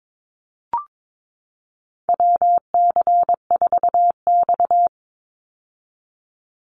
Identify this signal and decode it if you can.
Morse code, heard as a keyed tone, about 22 wpm, 700 Hz. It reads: WC4X
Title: Call Signs - 22wpm